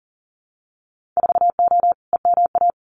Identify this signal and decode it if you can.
Morse code, heard as a keyed tone, about 40 wpm, 700 Hz. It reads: 4O EMA
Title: Winter Field Day Exchanges 40wpm